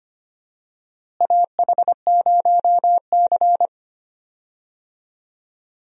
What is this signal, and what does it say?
Morse code, heard as a keyed tone, about 25 wpm, 700 Hz. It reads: AH0C
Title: USA Call Signs - 4 Characters - Repeated 2x - 25wpm